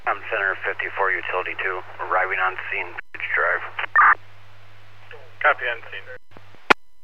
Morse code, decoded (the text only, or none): none